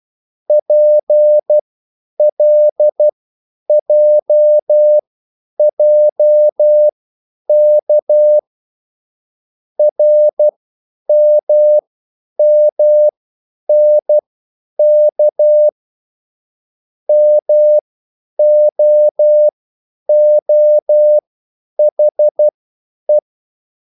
Morse code, heard as a keyed tone, about 12 wpm, spaced as for 9 wpm, 600 Hz. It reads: PLJJK RMMNK MOOHE